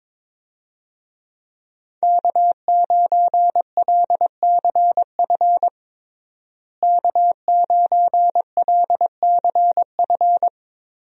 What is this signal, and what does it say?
Morse code, heard as a keyed tone, about 22 wpm, 700 Hz. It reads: K9LCF K9LCF
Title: USA Call Signs - 5 Characters - Repeated 2x - 22wpm